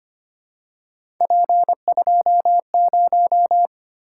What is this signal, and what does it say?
Morse code, heard as a keyed tone, about 25 wpm, 700 Hz. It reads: P20
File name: Three Letter-Number Combinations - 25wpm